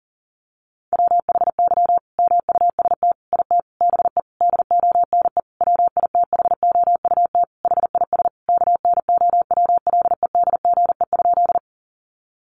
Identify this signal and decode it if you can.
Morse code, heard as a keyed tone, about 37 wpm, 700 Hz. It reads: WHY MUST IT BE DONE WITHOUT HIS KNOWLEDGE?
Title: Sentences from Top 600 Words 40wpm